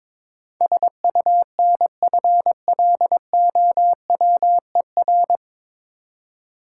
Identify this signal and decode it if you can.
Morse code, heard as a keyed tone, about 22 wpm, 700 Hz. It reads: SUNFLOWER
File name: Difficult Words with Hints - 22wpm